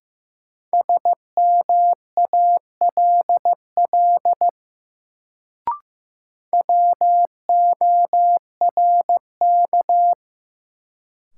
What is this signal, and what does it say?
Morse code, heard as a keyed tone, about 16 wpm, 700 Hz. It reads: SMALL WORK
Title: Sets of 2 Words - New Word Drill - 15wpm